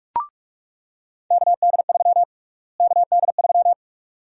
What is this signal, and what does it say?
Morse code, heard as a keyed tone, about 46 wpm, 700 Hz. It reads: KD3 KD3
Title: USA Call Sign Prefixes - Repeated 2x - 45wpm